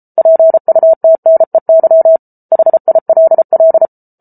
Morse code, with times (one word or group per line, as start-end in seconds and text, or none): 0.2-2.2 PUTNEY
2.5-3.9 HILL